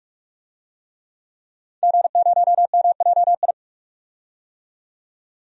Morse code, heard as a keyed tone, about 48 wpm, 700 Hz. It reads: G0MJI
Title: DXCC-100 Call Signs - 5 Characters - 45wpm